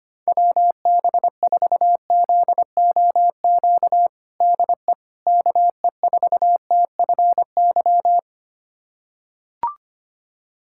Morse code, heard as a keyed tone, about 25 wpm, 700 Hz. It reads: WB4ZOQ DE KE4TFY
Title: USA Callsign DE Callsign 25wpm